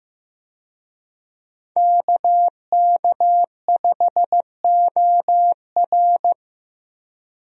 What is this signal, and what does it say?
Morse code, heard as a keyed tone, about 15 wpm, 700 Hz. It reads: KK5OR
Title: USA Call Signs - 5 Characters - Repeated 3x - 15wpm